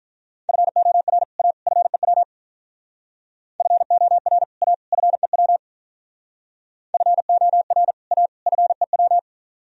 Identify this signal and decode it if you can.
Morse code, heard as a keyed tone, about 44 wpm, 700 Hz. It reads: FOR A FEW FOR A FEW FOR A FEW